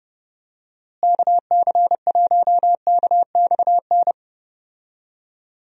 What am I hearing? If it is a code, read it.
Morse code, heard as a keyed tone, about 31 wpm, 700 Hz. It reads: KC1KXN
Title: USA Call Signs - 6 Characters - Repeated 2x - 30wpm